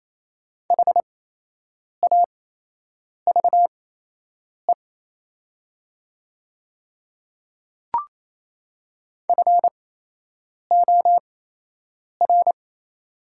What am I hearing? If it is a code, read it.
Morse code, heard as a keyed tone, about 28 wpm, spaced as for 8 wpm, 700 Hz. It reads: HAVE FOR
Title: 8x Character Spacing - Top 100 Words - 28wpm